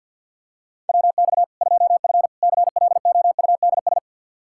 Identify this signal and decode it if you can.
Morse code, heard as a keyed tone, about 48 wpm, 700 Hz. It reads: WX 2F CLOUDS
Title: Practice QSO - Exchange 2 - Info - 50wpm